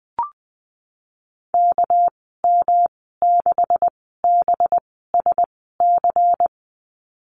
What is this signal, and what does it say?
Morse code, heard as a keyed tone, about 19 wpm, spaced as for 14 wpm, 700 Hz. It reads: KM6BSC
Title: 213 Lesson 34g - Number 8 - Call Signs 20wpm